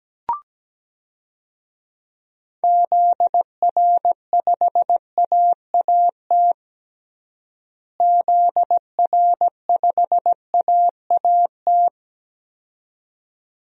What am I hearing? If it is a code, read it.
Morse code, heard as a keyed tone, about 17 wpm, 700 Hz. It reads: ZR5AAT ZR5AAT
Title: DXCC-100 Call Signs - 6 Characters - Repeated 2x - 17wpm